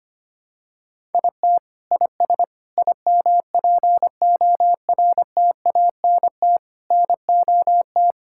Morse code, heard as a keyed tone, about 25 wpm, 700 Hz. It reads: IT IS IMPORTANT NOT
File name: Sets of 9 Words - 25wpm